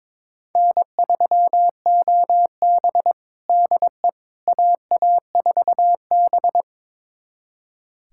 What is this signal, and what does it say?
Morse code, heard as a keyed tone, about 22 wpm, 700 Hz. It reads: N3OB DE AA4B
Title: USA Callsign DE Callsign - 4 Characters - 22wpm